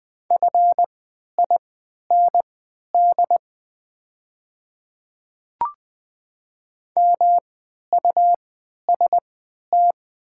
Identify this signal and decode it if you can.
Morse code, heard as a keyed tone, about 20 wpm, spaced as for 11 wpm, 700 Hz. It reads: FIND MUST
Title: 3x Character Spacing - Top 200 Words - 20wpm